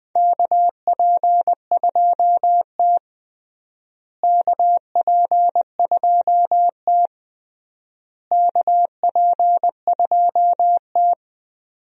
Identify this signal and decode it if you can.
Morse code, heard as a keyed tone, about 21 wpm, 700 Hz. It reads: KP2T KP2T KP2T
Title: USA Call Signs - 4 Characters - Repeated 3x - 20wpm